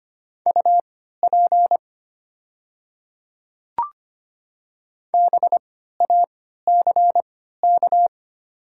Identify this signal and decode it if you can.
Morse code, heard as a keyed tone, about 25 wpm, spaced as for 14 wpm, 700 Hz. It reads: UP BACK
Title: 3x Character Spacing - Top 200 Words - 25wpm